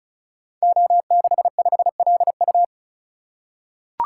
Morse code, heard as a keyed tone, about 36 wpm, 700 Hz. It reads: O65LU
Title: Code Groups - 5 Letter:Numbers - 35wpm